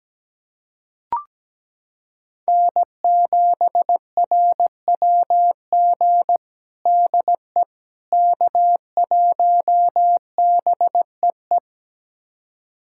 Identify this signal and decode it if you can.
Morse code, heard as a keyed tone, about 17 wpm, 700 Hz. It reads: N7RWG DE K1BEE
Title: USA Callsign DE Callsign - 5 Characters - 17wpm